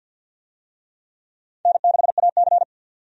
Morse code, heard as a keyed tone, about 52 wpm, 700 Hz. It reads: N6AC
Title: USA Call Signs - 4 Characters - 50wpm